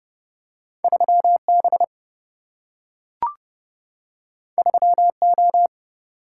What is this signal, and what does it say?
Morse code, heard as a keyed tone, about 30 wpm, 700 Hz. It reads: 3B 3O
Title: Single-Letter Single-Number 30wpm